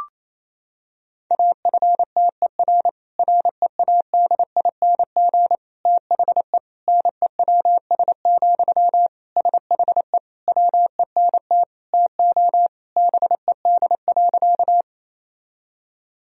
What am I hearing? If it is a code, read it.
Morse code, heard as a keyed tone, about 28 wpm, 700 Hz. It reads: AFTER READING THE NEWS, SHE WENT TO BED.
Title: Sentences from Top 1000 Words - Review Part 3 - 28wpm